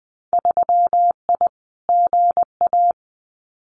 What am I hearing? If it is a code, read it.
Morse code, heard as a keyed tone, about 20 wpm, 700 Hz. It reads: 3I GA
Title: Winter Field Day Exchanges 20wpm